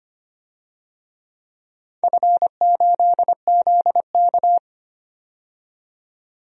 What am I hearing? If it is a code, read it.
Morse code, heard as a keyed tone, about 25 wpm, 700 Hz. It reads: F8ZK
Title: DXCC-100 Call Signs - 4 Characters - 25wpm